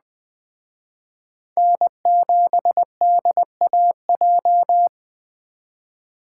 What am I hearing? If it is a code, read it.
Morse code, heard as a keyed tone, about 21 wpm, 700 Hz. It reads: N7DAJ